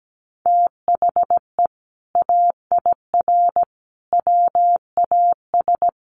Morse code, heard as a keyed tone, about 17 wpm, 700 Hz. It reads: THE AIR WAS
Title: Sentences from Top 1200 Words 17wpm